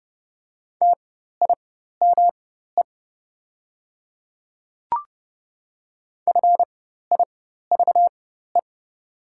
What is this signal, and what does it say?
Morse code, heard as a keyed tone, about 31 wpm, spaced as for 14 wpm, 700 Hz. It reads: TIME FIVE
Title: 4x Character Spacing - Top 500 Words - 30wpm